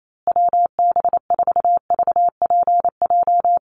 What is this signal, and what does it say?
Morse code, heard as a keyed tone, about 28 wpm, 700 Hz. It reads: WB4VPJ
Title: USA Call Signs Repeated 2x 28wpm